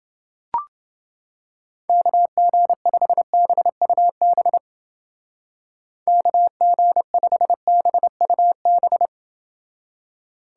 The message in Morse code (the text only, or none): KG5BUB KG5BUB